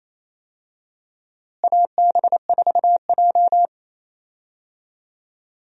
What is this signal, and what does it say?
Morse code, heard as a keyed tone, about 28 wpm, 700 Hz. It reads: AB4J
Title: USA Call Signs - 4 Characters - Repeated 3x - 28wpm